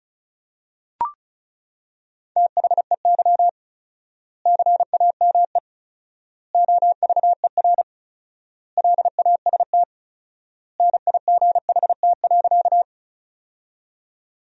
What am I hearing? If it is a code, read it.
Morse code, heard as a keyed tone, about 34 wpm, 700 Hz. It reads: THEY CAME OVER LAST NIGHT.
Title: 4x Word-Spacing - Sentences from Top 300 Words - 35wpm